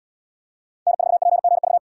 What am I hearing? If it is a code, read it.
Morse code, heard as a keyed tone, about 77 wpm, 700 Hz. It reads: N4PGV